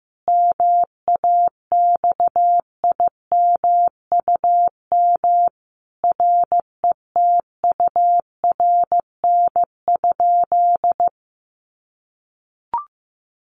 Morse code, heard as a keyed tone, about 14 wpm, 700 Hz. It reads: MAXIMUM RETURN?
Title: Sentences from Top 1300 Words 15wpm